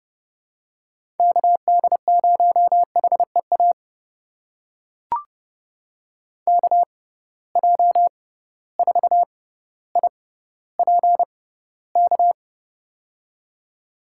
Morse code, heard as a keyed tone, about 30 wpm, spaced as for 15 wpm, 700 Hz. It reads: KD0HEA KJ4IPK
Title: USA Call Signs - 6 Characters - 6x Character Spacing - 30wpm